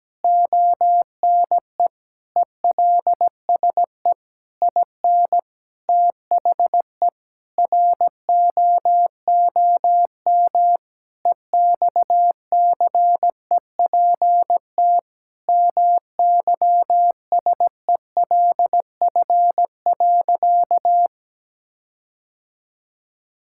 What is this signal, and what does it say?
Morse code, heard as a keyed tone, about 17 wpm, 700 Hz. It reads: ONE ELSE IN THE ROOM EXCEPT MYSELF.
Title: Sentences from Top 1600 Words 17wpm